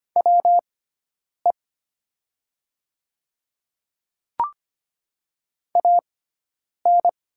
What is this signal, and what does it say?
Morse code, heard as a keyed tone, about 25 wpm, spaced as for 9 wpm, 700 Hz. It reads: WE AN
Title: 6x Character Spacing - Top 500 Words - 25wpm